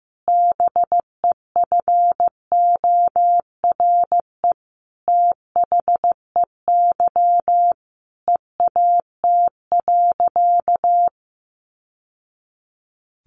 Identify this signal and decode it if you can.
Morse code, heard as a keyed tone, about 15 wpm, 700 Hz. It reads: BEFORE THEY EAT.
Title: Sentences from Top 700 Words 15wpm